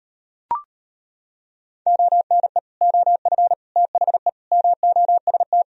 Morse code, heard as a keyed tone, about 33 wpm, 700 Hz. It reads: ONE OF THE MOST